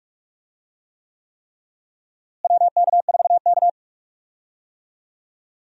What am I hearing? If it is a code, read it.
Morse code, heard as a keyed tone, about 46 wpm, 700 Hz. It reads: WK4K
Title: USA Call Signs - 4 Characters - Repeated 3x - 45wpm